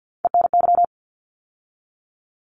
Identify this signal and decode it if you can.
Morse code, heard as a keyed tone, about 50 wpm, 700 Hz. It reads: ENY
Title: ARRL Sections 50wpm